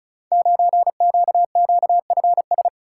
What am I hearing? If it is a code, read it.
Morse code, heard as a keyed tone, about 36 wpm, 700 Hz. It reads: 9QQFS
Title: Code Groups - 5 Letter:Numbers - 35wpm